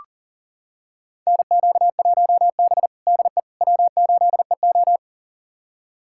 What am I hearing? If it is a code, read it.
Morse code, heard as a keyed tone, about 40 wpm, 700 Hz. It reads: NQ1B DE W8EO
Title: USA Callsign DE Callsign - 4 Characters - 40wpm